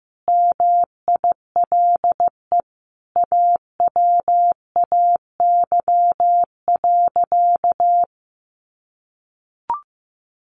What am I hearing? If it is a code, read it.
Morse code, heard as a keyed tone, about 15 wpm, 700 Hz. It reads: MILE AWAY.
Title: Sentences from Top 1300 Words 15wpm